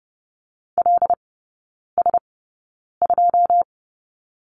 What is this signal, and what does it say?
Morse code, heard as a keyed tone, about 30 wpm, spaced as for 9 wpm, 700 Hz. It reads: LS2
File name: Lesson 19e - Letter w - Three Characters 30wpm